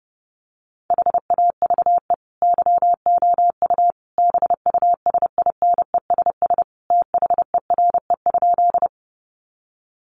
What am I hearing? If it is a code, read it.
Morse code, heard as a keyed tone, about 28 wpm, 700 Hz. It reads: HAVE YOU BUSINESS THERE?
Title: Sentences from Top 600 Words 30wpm